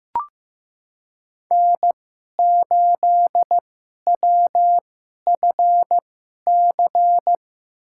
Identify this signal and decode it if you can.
Morse code, heard as a keyed tone, about 15 wpm, spaced as for 11 wpm, 700 Hz. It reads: N8WFC
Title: USA Call Signs - 5 Characters - 2x Character Spacing - 15wpm